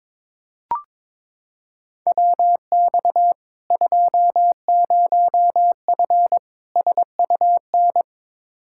Morse code, heard as a keyed tone, about 22 wpm, 700 Hz. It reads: WX 20F SUN
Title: Practice QSO - Exchange 2 - Info - 22wpm